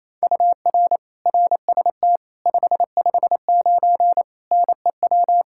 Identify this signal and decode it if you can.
Morse code, heard as a keyed tone, about 28 wpm, 700 Hz. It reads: UR RST 559 NEW
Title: Practice QSO - Exchange 1 - Beginning - 28wpm